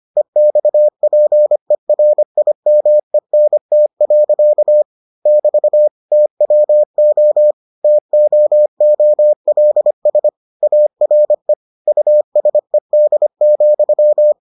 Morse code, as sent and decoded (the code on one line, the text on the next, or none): . -..- .--. .-. .. -- . -. - .-.-.- / -...- / - .-- --- / - --- --- .-.. ... / .- .-. / ..- ... -.. --..--
EXPERIMENT. <BT> TWO TOOLS ARE USED,